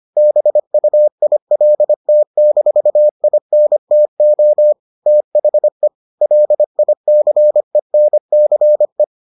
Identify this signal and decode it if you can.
Morse code, heard as a keyed tone, about 25 wpm, 600 Hz. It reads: BUILT-INTO THE LICENCE